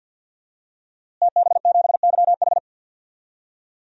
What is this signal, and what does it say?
Morse code, heard as a keyed tone, about 52 wpm, 700 Hz. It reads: TB7YH
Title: DXCC-100 Call Signs - 5 Characters - Repeated 3x - 50wpm